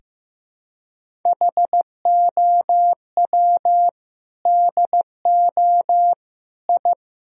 Morse code, heard as a keyed tone, about 16 wpm, 700 Hz. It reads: HOW DO I
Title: Sentences from Top 1000 Words - Review Part 1 - 15wpm